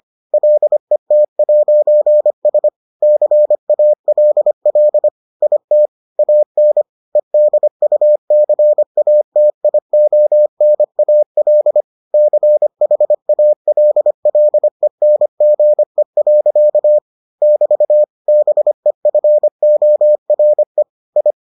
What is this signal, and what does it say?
Morse code, heard as a keyed tone, about 25 wpm, 600 Hz. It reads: LET'S CALL IT AN EDUCATIONAL CHALLENGE. <BT> BEFORE I